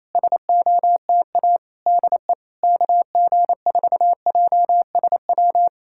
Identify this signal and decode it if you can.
Morse code, heard as a keyed tone, about 28 wpm, 700 Hz. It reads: SOTA DE KG4JSW
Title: SOTA QSO Practice 28wpm